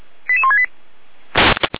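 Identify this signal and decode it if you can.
Five-tone selective call (ZVEI), heard as quick runs of five tones, tones of about 70 ms. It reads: 88168
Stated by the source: Five-tone ZVEI selective call 88169 received on SDR, fire brigade Feuerwehr Orlamünde.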